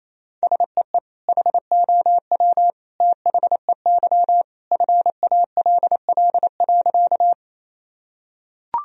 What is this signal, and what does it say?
Morse code, heard as a keyed tone, about 28 wpm, 700 Hz. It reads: SEE HOW THEY FALL.
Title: Sentences from Top 1000 Words - Review Part 2 - 28wpm